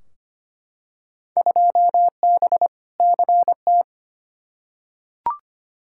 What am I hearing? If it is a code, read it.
Morse code, heard as a keyed tone, about 25 wpm, 700 Hz. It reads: 2B CT
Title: Field Day Exchanges Encore - 25wpm